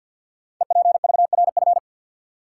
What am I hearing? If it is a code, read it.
Morse code, heard as a keyed tone, about 52 wpm, 700 Hz. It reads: EPVRF
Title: Code Groups - 5 Letters - 50wpm